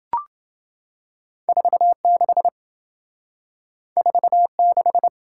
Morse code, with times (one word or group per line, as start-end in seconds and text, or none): 1.5-2.5 46
4.0-5.1 46